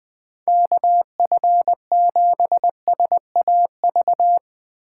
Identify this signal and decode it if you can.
Morse code, heard as a keyed tone, about 22 wpm, 700 Hz. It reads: KF7SAV
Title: Speed Racing - USA 2x3 Call Signs 20wpm